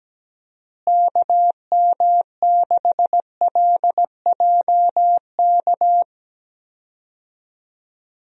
Morse code, heard as a keyed tone, about 17 wpm, 700 Hz. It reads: KM6LJK
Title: USA Call Signs - 6 Characters - Repeated 2x - 17wpm